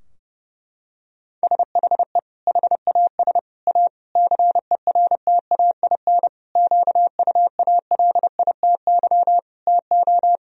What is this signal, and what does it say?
Morse code, heard as a keyed tone, about 30 wpm, 700 Hz. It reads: SHE HAS A CERTAIN QUALITY TO
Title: Sentences from Top 1000 Words - Review Part 4 - 30wpm